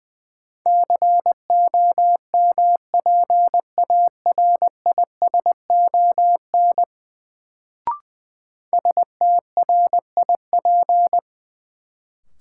Morse code, heard as a keyed tone, about 20 wpm, 700 Hz. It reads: COMPARISON STRIP
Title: Sentences from Top 1600 Words - New Word Drill - 20wpm